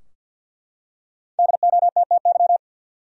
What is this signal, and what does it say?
Morse code, heard as a keyed tone, about 52 wpm, 700 Hz. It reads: DOTTY
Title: USA Common Names - Max 5 Letters - 50wpm